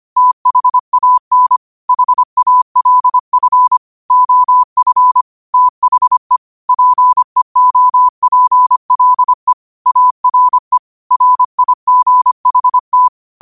Morse code, heard as a keyed tone, about 25 wpm, 1000 Hz. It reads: THAN HALF OF THE PEOPLE ARE RIGHT